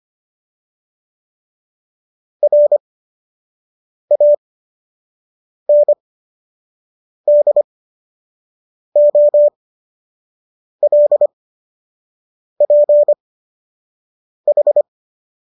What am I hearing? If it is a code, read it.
Morse code, heard as a keyed tone, about 25 wpm, spaced as for 6 wpm, 590 Hz. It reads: RANDOLPH